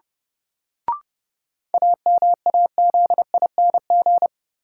Morse code, heard as a keyed tone, about 30 wpm, 700 Hz. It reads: AMAZING
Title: Sentences from Top 1500 Words - New Word Drill Alt - 30wpm